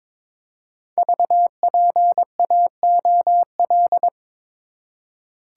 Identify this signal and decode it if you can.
Morse code, heard as a keyed tone, about 22 wpm, 700 Hz. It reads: VPAOL
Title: Code Groups - 5 Letters - 22wpm